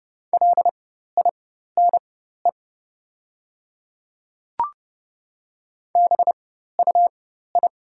Morse code, heard as a keyed tone, about 29 wpm, spaced as for 14 wpm, 700 Hz. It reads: LINE BUI
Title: 4x Character Spacing - Top 200 Words - 30wpm